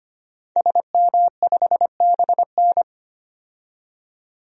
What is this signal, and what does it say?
Morse code, heard as a keyed tone, about 25 wpm, 700 Hz. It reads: SM5BN
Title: DXCC-100 Call Signs - 5 Characters - 25wpm